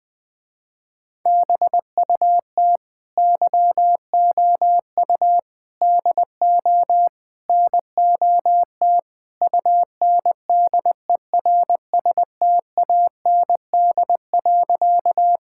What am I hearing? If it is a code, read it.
Morse code, heard as a keyed tone, about 20 wpm, 700 Hz. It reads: BUT YOU DO NOT UNDERSTAND.
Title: Sentences with 4-Word Phrases - Spoken Beginning - 20wpm